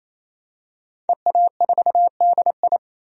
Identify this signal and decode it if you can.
Morse code, heard as a keyed tone, about 28 wpm, 700 Hz. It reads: EA4DI
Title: DXCC-100 Call Signs - 5 Characters - 28wpm